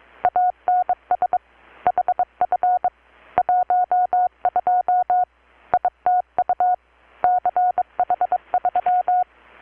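Morse code, heard as a keyed tone, about 23 wpm, 720 Hz. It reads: ANS HF 12 ITU CH3
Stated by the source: Morse from KPH's Night of Nights XX maritime radiotelegraphy event, received in Chiapas, Mexico about 3,800 km away on a Kenwood R-600 with a grounded 10 m long-wire antenna.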